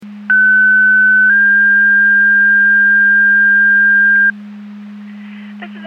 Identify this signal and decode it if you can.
Two-tone page, heard as a one-second tone, then a three-second tone, about 1500 then 1600 Hz, with speech. It is heard from 0.3 to 4.3 s.